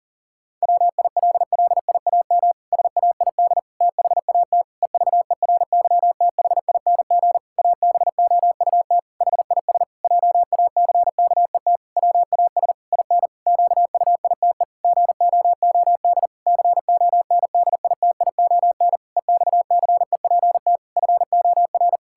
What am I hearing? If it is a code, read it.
Morse code, heard as a keyed tone, about 40 wpm, 700 Hz. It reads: WILLIAM SAID THAT EVERYTHING ABOUT HIS JACKET WAS IN QUITE GOOD CONDITION EXCEPT FOR
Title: Pangram Challenge v2 - 40wpm